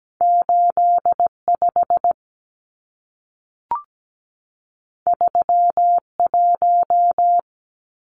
Morse code, heard as a keyed tone, about 17 wpm, 700 Hz. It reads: 85 31
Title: Numbers - 2-Digit 17wpm